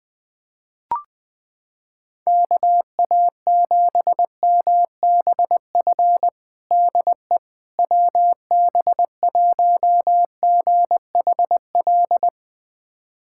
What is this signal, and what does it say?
Morse code, heard as a keyed tone, about 21 wpm, 700 Hz. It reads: KA7MBF DE WB1GHL
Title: USA Callsign DE Callsign - 6 Characters - 20wpm